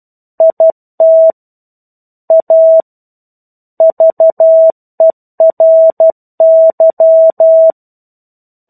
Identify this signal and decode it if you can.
Morse code, heard as a keyed tone, about 12 wpm, 650 Hz. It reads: IT A VERY
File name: 01-Chapter-01 12 WPM